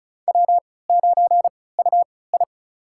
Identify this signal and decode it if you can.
Morse code, heard as a keyed tone, about 36 wpm, spaced as for 20 wpm, 700 Hz. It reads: W9UI
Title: USA Call Signs - 3x Character Spacing - 35wpm